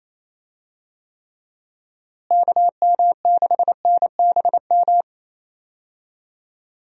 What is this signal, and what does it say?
Morse code, heard as a keyed tone, about 28 wpm, 700 Hz. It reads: KM6NBM
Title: USA Call Signs - 6 Characters - Repeated 2x - 28wpm